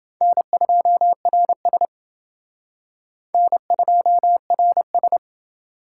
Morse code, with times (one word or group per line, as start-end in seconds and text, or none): 0.2-1.9 N2RS
3.3-5.2 N2RS